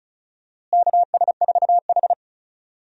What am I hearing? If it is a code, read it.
Morse code, heard as a keyed tone, about 36 wpm, 700 Hz. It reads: KS4H
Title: USA Call Signs - 4 Characters - Repeated 3x - 35wpm